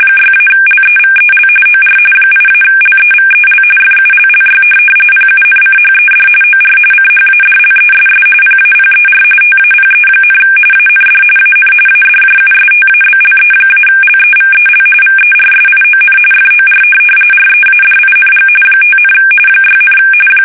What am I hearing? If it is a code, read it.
RTTY, about 45.45 baud. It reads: 6-LEVEL OR 8-LEVEL FULLY PUNCHED OR CHADLESS PAPER TAPE, AND AN ADJUSTABLE-LEVEL VERSION FOR READING 5, 6,
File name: fsk850 / teletype-corp